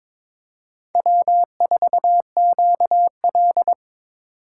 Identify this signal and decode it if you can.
Morse code, heard as a keyed tone, about 22 wpm, 700 Hz. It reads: W4QL
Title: USA Call Signs - 4 Characters - 2x Character Spacing - 22wpm